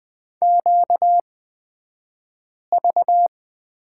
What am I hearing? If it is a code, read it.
Morse code, heard as a keyed tone, about 20 wpm, spaced as for 5 wpm, 690 Hz. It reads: QV
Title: 235 Lesson 38d - Letter q - Two Characters 20wpm